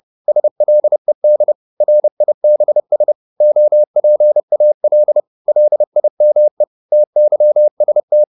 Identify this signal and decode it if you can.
Morse code, heard as a keyed tone, about 30 wpm, 600 Hz. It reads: SLED RIBS OPAL LIME TYST
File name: Korte ord DK 30wpm